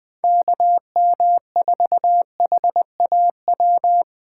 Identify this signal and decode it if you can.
Morse code, heard as a keyed tone, about 20 wpm, 700 Hz. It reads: KM4HAW